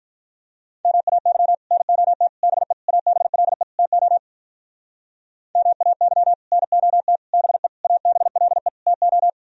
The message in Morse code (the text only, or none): MAY NOT BE ABLE TO MAY NOT BE ABLE TO